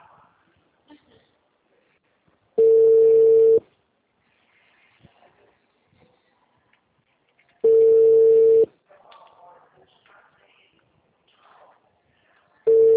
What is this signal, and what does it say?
Ringback tone (China, 450 Hz), with speech.